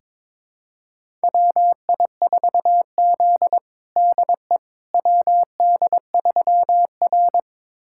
Morse code, heard as a keyed tone, about 22 wpm, 700 Hz. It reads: WI4Z DE WD3R
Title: USA Callsign DE Callsign - 4 Characters - 22wpm